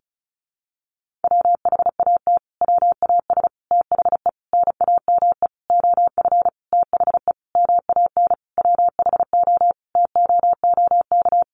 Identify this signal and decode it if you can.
Morse code, heard as a keyed tone, about 34 wpm, 700 Hz. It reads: WHAT WAS THE NAME OF THE MAN WHO TOOK
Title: Sentences from Top 300 Words 35wpm